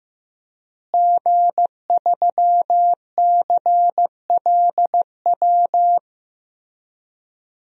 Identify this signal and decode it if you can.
Morse code, heard as a keyed tone, about 15 wpm, 700 Hz. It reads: G3CLW
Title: DXCC-100 Call Signs - 5 Characters - Repeated 3x - 15wpm